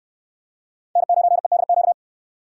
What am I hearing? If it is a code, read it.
Morse code, heard as a keyed tone, about 61 wpm, 700 Hz. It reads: N9EDQ